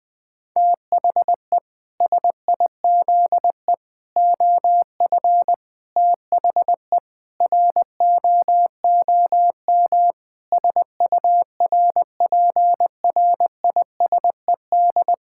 Morse code, heard as a keyed tone, about 20 wpm, 700 Hz. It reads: THE SIZE OF THE ROOM SURPRISED
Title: Sentences with 4-Word Phrases - Spoken Beginning - 20wpm